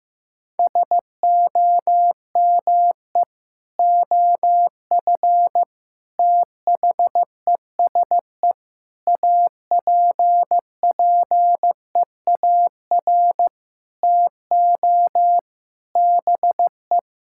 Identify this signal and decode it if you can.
Morse code, heard as a keyed tone, about 15 wpm, 700 Hz. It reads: SOME OF THESE APPEAR TO BE
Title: Sentences from Top 1000 Words - Review Part 6 - 15wpm